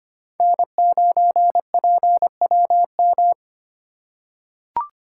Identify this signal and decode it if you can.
Morse code, heard as a keyed tone, about 25 wpm, 700 Hz. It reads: N9PWM
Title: USA Call Signs - 5 Characters - 3x Character Spacing - 25wpm